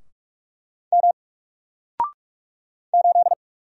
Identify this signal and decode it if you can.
Morse code, heard as a keyed tone, about 44 wpm, 700 Hz. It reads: M 8
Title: Single Letter-Number 45wpm